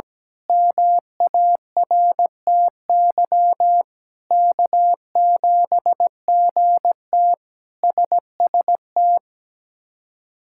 Morse code, heard as a keyed tone, about 17 wpm, 700 Hz. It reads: MARTY K7GT SST
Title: K1USN SST QSO Practice - 17wpm